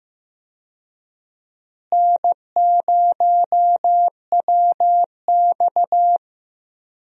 Morse code, heard as a keyed tone, about 15 wpm, 700 Hz. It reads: N0WX
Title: USA Call Signs - 4 Characters - Repeated 3x - 15wpm